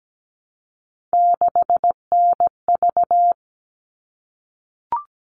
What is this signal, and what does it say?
Morse code, heard as a keyed tone, about 17 wpm, 700 Hz. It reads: 6NV
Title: All Cut Numbers - 3-Digit 17wpm